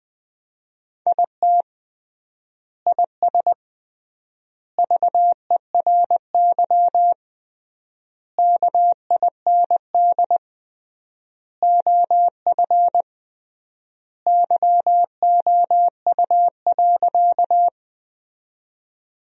Morse code, heard as a keyed tone, about 20 wpm, 700 Hz. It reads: IT IS VERY KIND OF YOU.
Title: 3x Word-Spacing - Sentences from Top 200 Words - 20wpm